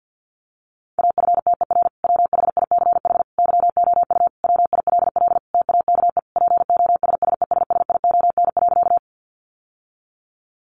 Morse code, heard as a keyed tone, about 50 wpm, 700 Hz. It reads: AFTER WHICH YOU WILL TAKE POSSESSION.
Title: Sentences from Top 1400 Words 50wpm